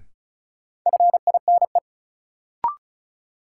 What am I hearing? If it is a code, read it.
Morse code, heard as a keyed tone, about 36 wpm, 700 Hz. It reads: FINE
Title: Sentences from Top 500 Words - New Word Drill - 35wpm